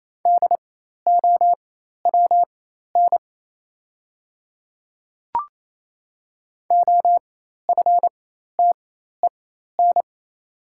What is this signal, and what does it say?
Morse code, heard as a keyed tone, about 28 wpm, spaced as for 13 wpm, 690 Hz. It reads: DOWN OFTEN
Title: 4x Character Spacing - Top 500 Words - 28wpm